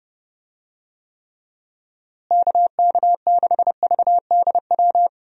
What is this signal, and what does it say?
Morse code, heard as a keyed tone, about 30 wpm, 700 Hz. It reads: KK6VDW
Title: USA Call Signs - 6 Characters - Repeated 3x - 30wpm